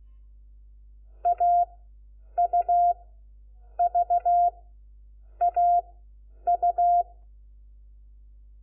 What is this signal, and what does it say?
Morse code, heard as a keyed tone, about 16 wpm, 700 Hz. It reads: A U V A U
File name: Lecciones / CLASE Nº4